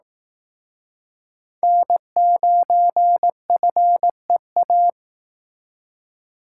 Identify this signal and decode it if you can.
Morse code, heard as a keyed tone, about 18 wpm, 700 Hz. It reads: N9FEA
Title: Speed Racing - USA 1x3 Call Signs 15wpm